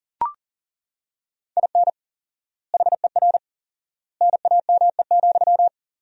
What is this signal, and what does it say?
Morse code, heard as a keyed tone, about 40 wpm, 700 Hz. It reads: IN HER NAME,
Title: 4x Word-Spacing - Sentences from Top 500 Words - 40wpm